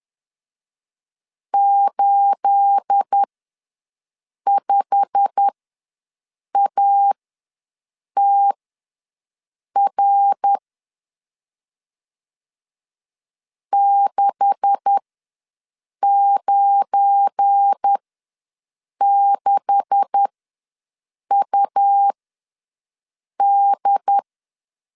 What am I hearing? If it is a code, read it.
Morse code, heard as a keyed tone, about 10 wpm, spaced as for 6 wpm, 800 Hz. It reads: 85ATR 696UD
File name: LEZIONI / 9 286 ws 10wpm 4032